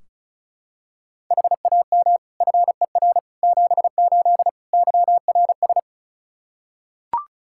Morse code, heard as a keyed tone, about 34 wpm, 700 Hz. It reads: HAM FER 78 YRS
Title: Practice QSO - Exchange 3 - More Info - 35wpm